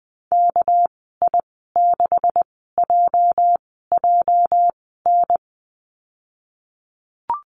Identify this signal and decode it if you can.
Morse code, heard as a keyed tone, about 20 wpm, spaced as for 14 wpm, 700 Hz. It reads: KI6JJN
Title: 213 Lesson 34g - Number 8 - Call Signs 20wpm